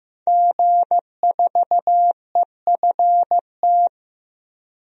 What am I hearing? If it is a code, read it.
Morse code, heard as a keyed tone, about 15 wpm, 700 Hz. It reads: G4EFT
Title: DXCC-100 Call Signs - 5 Characters - 15wpm